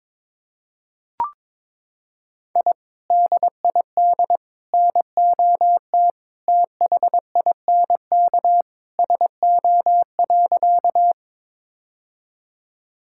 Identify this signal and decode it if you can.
Morse code, heard as a keyed tone, about 22 wpm, 700 Hz. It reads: I DID NOT THINK SO.
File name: practice / 22wpm / Sentences with 2-Word Phrases - 22wpm